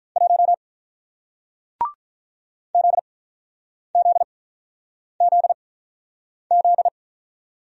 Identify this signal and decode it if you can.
Morse code, heard as a keyed tone, about 47 wpm, spaced as for 9 wpm, 700 Hz. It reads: 1 ZZZZ